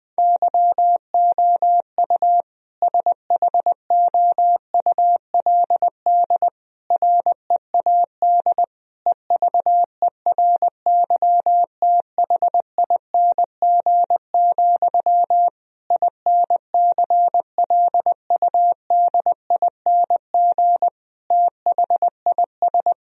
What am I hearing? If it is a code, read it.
Morse code, heard as a keyed tone, about 19 wpm, 700 Hz. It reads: YOU SHOULD READ EVERYTHING, INCLUDING THIS
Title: Sentences from Top 1000 Words - Review Part 4 - 20wpm